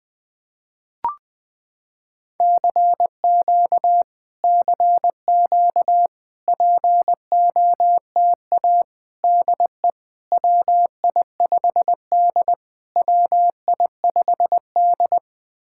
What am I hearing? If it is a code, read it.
Morse code, heard as a keyed tone, about 20 wpm, 700 Hz. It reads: CQ CQ POTA DE WI5D WI5D
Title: POTA QSO Practice - 20wpm